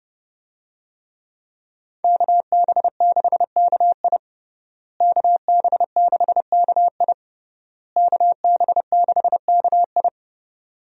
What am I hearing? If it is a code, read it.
Morse code, heard as a keyed tone, about 32 wpm, 700 Hz. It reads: KB6KI KB6KI KB6KI
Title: USA Call Signs - 5 Characters - Repeated 3x - 30wpm